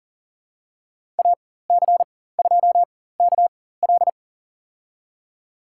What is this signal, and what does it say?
Morse code, heard as a keyed tone, about 40 wpm, spaced as for 19 wpm, 700 Hz. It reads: AC2KL
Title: USA Call Signs - 5 Characters - 4x Character Spacing - 40wpm